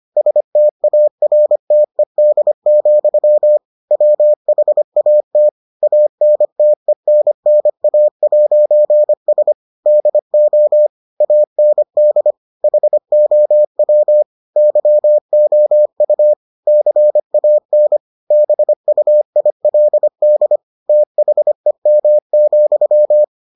Morse code, heard as a keyed tone, about 25 wpm, 600 Hz. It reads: STARTED, WHAT ANTENNA'S DO AND HOW YOU CAN BUILD THEM,